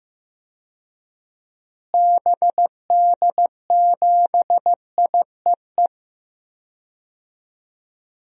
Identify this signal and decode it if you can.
Morse code, heard as a keyed tone, about 15 wpm, 700 Hz. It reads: BD7IEE
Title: DXCC-100 Call Signs - 6 Characters - Repeated 2x - 15wpm